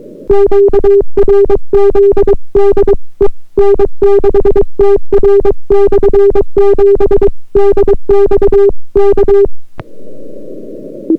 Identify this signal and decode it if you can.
Morse code, heard as a keyed tone, about 22 wpm, 400 Hz. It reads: QRZ DE N6TR/7 DX K